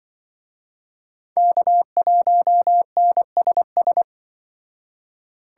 Morse code, heard as a keyed tone, about 24 wpm, 700 Hz. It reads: K1NSS
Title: Speed Racing - USA 1x3 Call Signs 20wpm